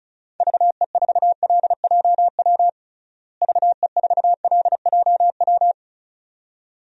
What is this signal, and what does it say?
Morse code, heard as a keyed tone, about 36 wpm, 700 Hz. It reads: VE4LJW VE4LJW
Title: DXCC-100 Call Signs - 6 Characters - Repeated 2x - 35wpm